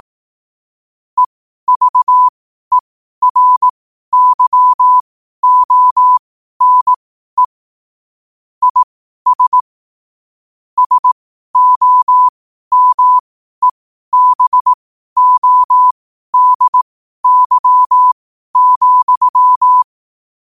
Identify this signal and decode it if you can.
Morse code, heard as a keyed tone, about 18 wpm, spaced as for 12 wpm, 1000 Hz. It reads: EVERYONE IS SOMEBODY,
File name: qotdfiles / 12wpm